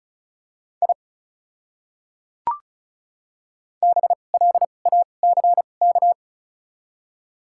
Morse code, heard as a keyed tone, about 36 wpm, spaced as for 25 wpm, 700 Hz. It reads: I BLACK
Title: 2x Character Spacing - Top 500 Words - 35wpm